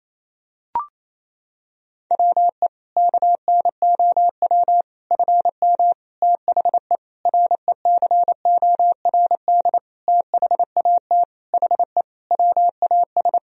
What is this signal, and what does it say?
Morse code, heard as a keyed tone, about 28 wpm, 700 Hz. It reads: WE KNOW FM THE RECORD THAT HE WAS
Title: Sentences from Top 500 Words and Abbreviations 28wpm